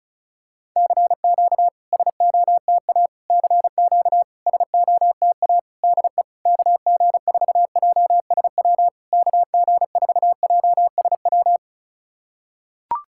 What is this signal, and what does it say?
Morse code, heard as a keyed tone, about 36 wpm, 690 Hz. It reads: CQ SOTA CQ SOTA DE KG4JSW KG4JSW
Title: SOTA QSO Practice 35wpm